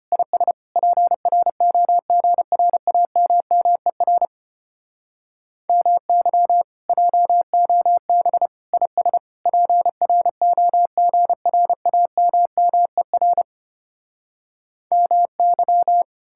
Morse code, heard as a keyed tone, about 33 wpm, 700 Hz. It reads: IS PROGRAMMER MY JOB IS PROGRAMMER MY